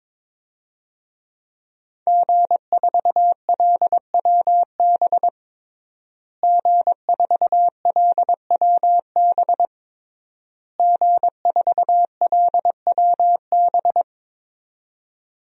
Morse code, heard as a keyed tone, about 22 wpm, 700 Hz. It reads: G4LWB G4LWB G4LWB
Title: DXCC-100 Call Signs - 5 Characters - Repeated 3x - 22wpm